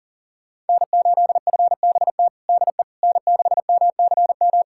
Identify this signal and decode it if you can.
Morse code, heard as a keyed tone, about 40 wpm, 690 Hz. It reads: N8FBT DE N6MCM
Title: USA Callsign DE Callsign - 5 Characters - 40wpm